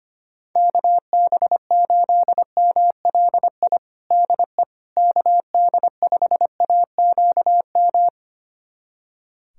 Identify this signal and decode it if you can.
Morse code, heard as a keyed tone, about 25 wpm, 700 Hz. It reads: KB8MLI DE KD5AQM
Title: USA Callsign DE Callsign 25wpm